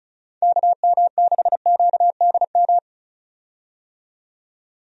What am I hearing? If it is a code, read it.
Morse code, heard as a keyed tone, about 34 wpm, 700 Hz. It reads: KM6QDM
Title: USA Call Signs - 6 Characters - 35wpm